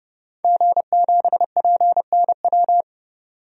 Morse code, heard as a keyed tone, about 31 wpm, 700 Hz. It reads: G7PNW
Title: DXCC-100 Call Signs - 5 Characters - 30wpm